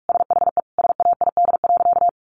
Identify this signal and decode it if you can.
Morse code, heard as a keyed tone, about 44 wpm, 700 Hz. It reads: SHE SAID.
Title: Sentences from Top 400 Words 45wpm